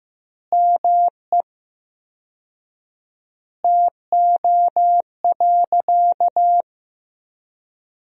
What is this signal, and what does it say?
Morse code, heard as a keyed tone, about 16 wpm, 700 Hz. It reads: ME TO.
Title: 4x Word-Spacing - Sentences from Top 400 Words - 15wpm